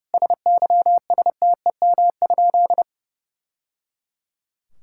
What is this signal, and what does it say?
Morse code, heard as a keyed tone, about 30 wpm, 700 Hz. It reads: SYSTEM?
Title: Sentences from Top 1000 Words - Review Part 5 - 30wpm